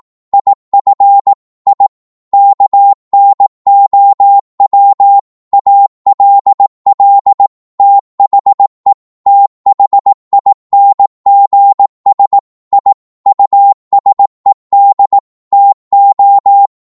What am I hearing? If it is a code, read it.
Morse code, heard as a keyed tone, about 18 wpm, 800 Hz. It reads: IF I KNOW ALL THE THINGS I USED TO